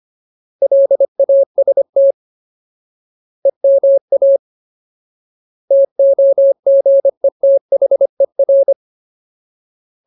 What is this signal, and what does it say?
Morse code, heard as a keyed tone, about 25 wpm, 550 Hz. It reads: LAST EMA TOGETHER
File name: Ultimate Word List / 25WPM Ultimate